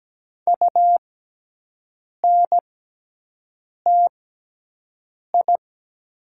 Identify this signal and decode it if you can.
Morse code, heard as a keyed tone, about 17 wpm, spaced as for 6 wpm, 700 Hz. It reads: UNTI
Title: 6x Character Spacing - Top 500 Words - 17wpm